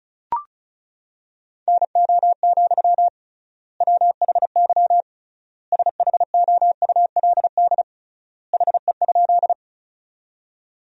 Morse code, heard as a keyed tone, about 36 wpm, 700 Hz. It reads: NO, WHY SHOULD HE?
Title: 3x Word-Spacing - Sentences from Top 300 Words - 35wpm